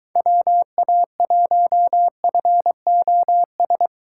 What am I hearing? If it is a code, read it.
Morse code, heard as a keyed tone, about 22 wpm, 700 Hz. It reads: WA1FOS